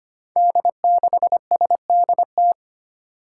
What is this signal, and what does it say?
Morse code, heard as a keyed tone, about 25 wpm, 700 Hz. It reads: D6SDT
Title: Five Challenging Letter-Number Combinations 25wpm